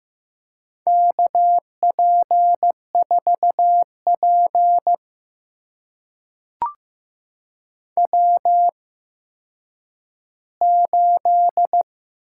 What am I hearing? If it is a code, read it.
Morse code, heard as a keyed tone, about 15 wpm, spaced as for 9 wpm, 700 Hz. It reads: KP4P W8
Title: USA Call Signs - 4 Characters - 8x Character Spacing - 15wpm